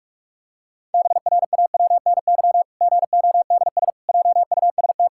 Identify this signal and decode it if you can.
Morse code, heard as a keyed tone, about 42 wpm, 700 Hz. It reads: BRAWNY GODS JUST
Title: Pangram Challenge v2 - 45wpm